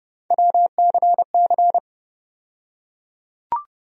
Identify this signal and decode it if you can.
Morse code, heard as a keyed tone, about 30 wpm, 700 Hz. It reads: WCC
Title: USA Call Sign Postfixes - 3 Letters 30wpm